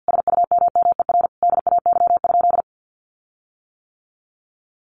Morse code, heard as a keyed tone, about 48 wpm, 700 Hz. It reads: SUMMER DAY?
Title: Sentences from Top 1500 Words 50wpm